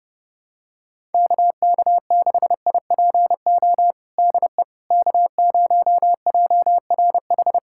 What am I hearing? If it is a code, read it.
Morse code, heard as a keyed tone, about 30 wpm, 700 Hz. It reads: KK6IPO DE K0JRH
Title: USA Callsign DE Callsign 30wpm